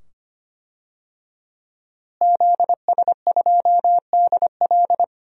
Morse code, heard as a keyed tone, about 25 wpm, 700 Hz. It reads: ZS2DL
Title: DXCC-100 Call Signs - 5 Characters - Repeated 2x - 25wpm